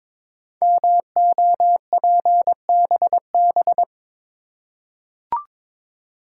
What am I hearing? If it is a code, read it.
Morse code, heard as a keyed tone, about 22 wpm, 700 Hz. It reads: MOPBB
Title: Code Groups - 5 Letters - 22wpm